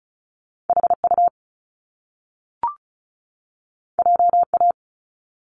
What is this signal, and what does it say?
Morse code, heard as a keyed tone, about 36 wpm, 700 Hz. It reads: HU JA
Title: Top Two Letter Combinations in Words 35wpm